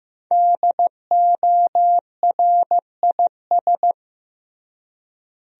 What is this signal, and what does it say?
Morse code, heard as a keyed tone, about 15 wpm, 700 Hz. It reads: DORIS
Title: USA Common Names - 15wpm